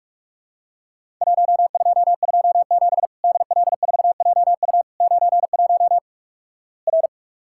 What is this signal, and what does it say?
Morse code, heard as a keyed tone, about 46 wpm, 700 Hz. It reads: 1227 DL4JU 91 R
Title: Worked All Europe DX Contest - QTC Exchange - 45wpm